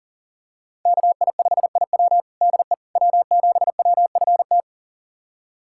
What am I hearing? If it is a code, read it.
Morse code, heard as a keyed tone, about 37 wpm, 700 Hz. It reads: KI5IW DE W7WFT
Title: USA Callsign DE Callsign 40wpm